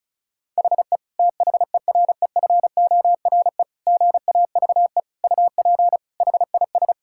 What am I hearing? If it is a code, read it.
Morse code, heard as a keyed tone, about 34 wpm, 700 Hz. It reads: HE THEREFORE GAVE UP HIS
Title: Sentences from Top 1000 Words - Review Part 5 - 35wpm